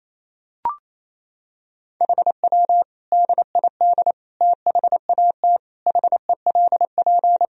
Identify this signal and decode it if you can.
Morse code, heard as a keyed tone, about 28 wpm, 700 Hz. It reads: HW DID THAT HELP
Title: Sentences from Top 200 Words and Abbreviations 28wpm